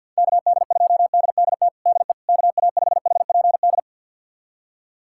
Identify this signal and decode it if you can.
Morse code, heard as a keyed tone, about 52 wpm, 700 Hz. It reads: KD2DDT DE KA5HPD
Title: USA Callsign DE Callsign - 6 Characters - 50wpm